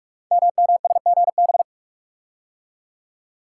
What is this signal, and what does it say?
Morse code, heard as a keyed tone, about 44 wpm, 700 Hz. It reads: MMSGB
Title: Code Groups - 5 Letters - 45wpm